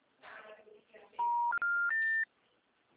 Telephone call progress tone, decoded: special information tone

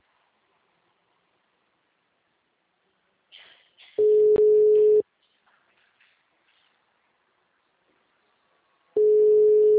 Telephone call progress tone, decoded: ringback tone

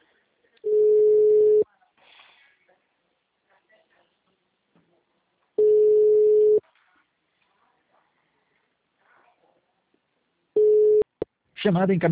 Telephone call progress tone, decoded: ringback tone

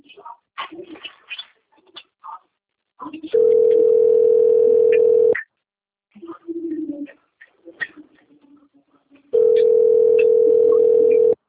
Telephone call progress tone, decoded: ringback tone

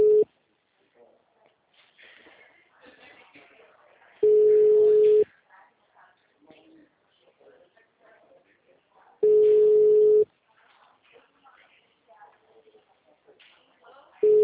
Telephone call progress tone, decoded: ringback tone

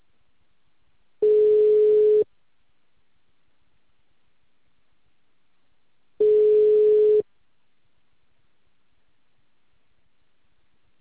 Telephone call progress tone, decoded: ringback tone

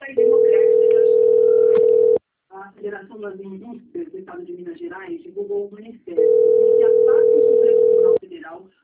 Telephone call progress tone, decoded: ringback tone

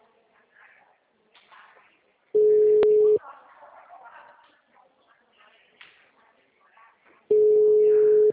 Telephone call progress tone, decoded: ringback tone

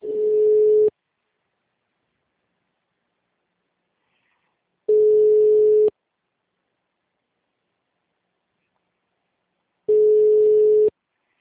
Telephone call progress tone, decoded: ringback tone

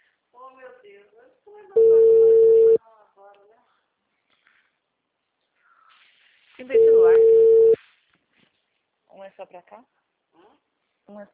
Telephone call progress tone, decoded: ringback tone

